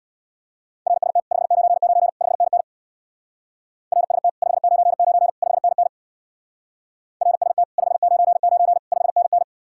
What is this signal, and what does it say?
Morse code, heard as a keyed tone, about 66 wpm, 700 Hz. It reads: RST 599 5NN RST 599 5NN RST 599 5NN